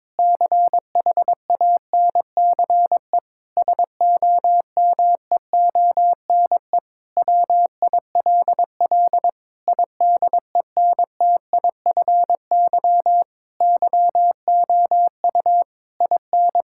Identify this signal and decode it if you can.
Morse code, heard as a keyed tone, about 22 wpm, 700 Hz. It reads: CHANCE SOMEONE WILL IDENTIFY YOU IN
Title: Sentences from Top 1000 Words - Review Part 6 - 22wpm